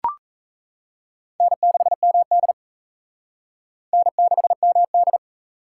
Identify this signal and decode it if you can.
Morse code, heard as a keyed tone, about 37 wpm, 700 Hz. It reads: N6MD N6MD